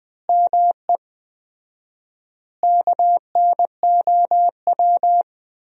Morse code, heard as a keyed tone, about 20 wpm, 700 Hz. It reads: ME KNOW
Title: 4x Word-Spacing - Sentences from Top 300 Words - 20wpm